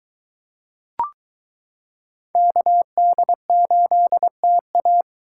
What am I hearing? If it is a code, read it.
Morse code, heard as a keyed tone, about 22 wpm, 700 Hz. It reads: KD8TA